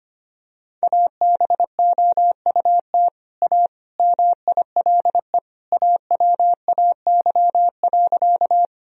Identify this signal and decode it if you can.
Morse code, heard as a keyed tone, about 25 wpm, 700 Hz. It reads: ABOUT A MILE AWAY.
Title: Sentences from Top 1000 Words - Review Part 1 - 25wpm